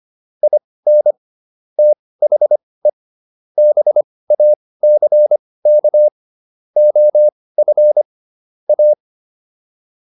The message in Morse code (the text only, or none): IN THE BACK OF A